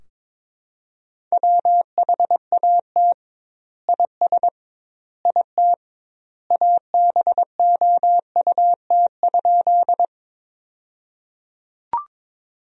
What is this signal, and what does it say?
Morse code, heard as a keyed tone, about 22 wpm, 700 Hz. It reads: WHAT IS IT ABOUT?
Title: 2x Word-Spacing - Sentences from Top 100 Words - 22wpm